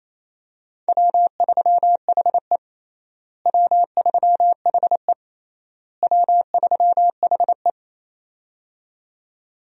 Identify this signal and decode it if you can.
Morse code, heard as a keyed tone, about 28 wpm, 700 Hz. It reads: W3HE W3HE W3HE
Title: USA Call Signs - 4 Characters - Repeated 3x - 28wpm